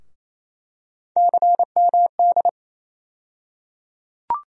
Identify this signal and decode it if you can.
Morse code, heard as a keyed tone, about 28 wpm, 700 Hz. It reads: CMD
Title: USA Call Sign Postfixes - 3 Letters 28wpm